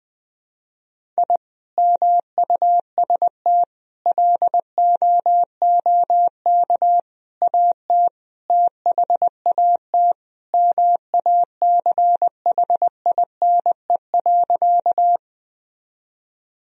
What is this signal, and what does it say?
Morse code, heard as a keyed tone, about 20 wpm, 700 Hz. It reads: I MUST LOOK AT THAT MACHINE.
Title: Sentences from Top 1000 Words - Review Part 2 - 20wpm